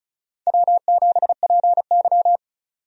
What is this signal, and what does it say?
Morse code, heard as a keyed tone, about 36 wpm, 700 Hz. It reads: W7PY
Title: USA Call Signs - 4 Characters - 35wpm